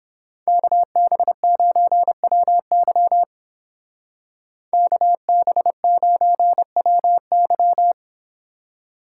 Morse code, heard as a keyed tone, about 29 wpm, 700 Hz. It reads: KB9WY KB9WY